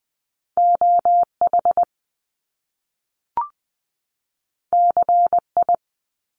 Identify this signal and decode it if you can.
Morse code, heard as a keyed tone, about 20 wpm, 700 Hz. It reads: OH CI
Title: Top Two Letter Combinations in Words 20wpm